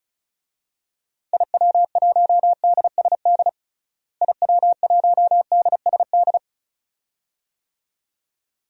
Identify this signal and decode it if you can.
Morse code, heard as a keyed tone, about 34 wpm, 700 Hz. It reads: IW1DSD IW1DSD
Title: DXCC-100 Call Signs - 6 Characters - Repeated 2x - 35wpm